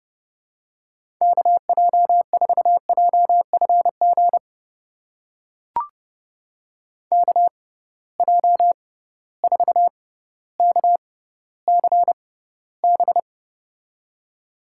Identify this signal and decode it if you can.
Morse code, heard as a keyed tone, about 30 wpm, spaced as for 15 wpm, 700 Hz. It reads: KJ4JFG KJ4KCB
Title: USA Call Signs - 6 Characters - 6x Character Spacing - 30wpm